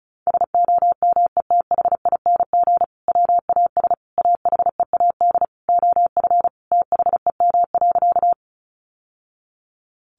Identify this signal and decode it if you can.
Morse code, heard as a keyed tone, about 34 wpm, 700 Hz. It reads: SOMETHING WAS AHEAD OF THEM.
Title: Sentences from Top 1100 Words 35wpm